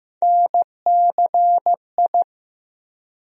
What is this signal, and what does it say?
Morse code, heard as a keyed tone, about 15 wpm, 700 Hz. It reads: NCI
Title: USA Call Sign Postfixes - 3 Letters 15wpm